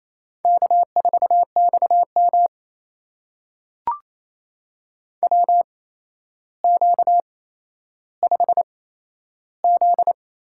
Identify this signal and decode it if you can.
Morse code, heard as a keyed tone, about 28 wpm, spaced as for 12 wpm, 700 Hz. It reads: K4XM WQ5Z
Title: USA Call Signs - 4 Characters - 8x Character Spacing - 28wpm